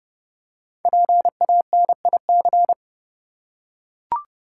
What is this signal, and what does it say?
Morse code, heard as a keyed tone, about 29 wpm, 700 Hz. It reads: PANIC
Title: Five Letter Words - 30wpm